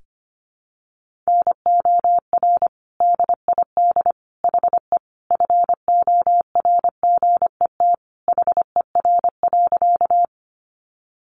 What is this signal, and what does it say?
Morse code, heard as a keyed tone, about 25 wpm, 700 Hz. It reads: NOR DID HE FORGET HER.
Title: Sentences from Top 1500 Words 25wpm